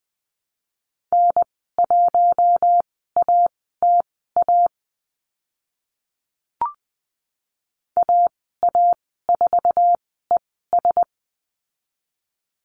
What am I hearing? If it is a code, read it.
Morse code, heard as a keyed tone, about 20 wpm, spaced as for 15 wpm, 700 Hz. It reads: N1ATA AA4ES
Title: 076 Lesson 13g - Letter L - Call Signs 20wpm